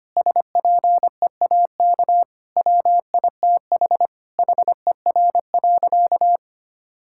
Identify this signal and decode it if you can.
Morse code, heard as a keyed tone, about 25 wpm, 700 Hz. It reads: SPEAK WITH HER.
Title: Sentences from Top 1600 Words 25wpm